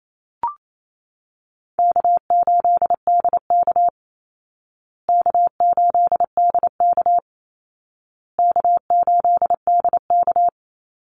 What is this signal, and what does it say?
Morse code, heard as a keyed tone, about 28 wpm, 700 Hz. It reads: K8DK K8DK K8DK
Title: Easy Call Signs Repeated 3x - 28wpm